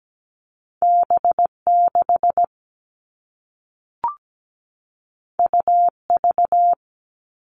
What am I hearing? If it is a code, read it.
Morse code, heard as a keyed tone, about 17 wpm, 700 Hz. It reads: B6 UV
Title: All Cut Numbers - 2-Digit 17wpm